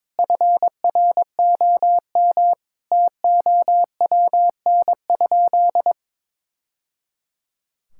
Morse code, heard as a keyed tone, about 22 wpm, 700 Hz. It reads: FROM TOWN?
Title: Sentences from Top 1000 Words - Review Part 5 - 22wpm